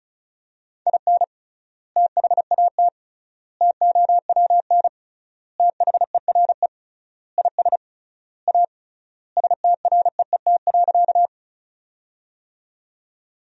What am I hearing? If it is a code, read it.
Morse code, heard as a keyed tone, about 36 wpm, 700 Hz. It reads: IN THAT TOWN THERE IS A STREET.
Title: 3x Word-Spacing - Sentences from Top 500 Words - 35wpm